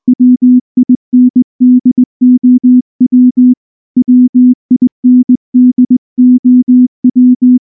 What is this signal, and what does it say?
Morse code, heard as a keyed tone, about 19 wpm, 250 Hz. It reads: WINDOW WINDOW
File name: morse-training-20WPM